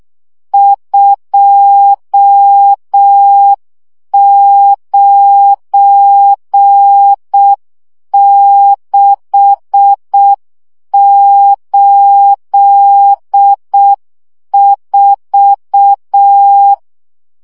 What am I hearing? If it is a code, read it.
Morse code, heard as a keyed tone, about 6 wpm, 800 Hz. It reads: 29684
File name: morze vizsga 2011-1 5